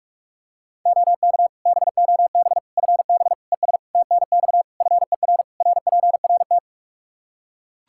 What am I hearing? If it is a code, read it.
Morse code, heard as a keyed tone, about 41 wpm, 700 Hz. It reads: OK BOB FB ES TNX FER RPRT
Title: Practice QSO - Exchange 2 - Info - 45wpm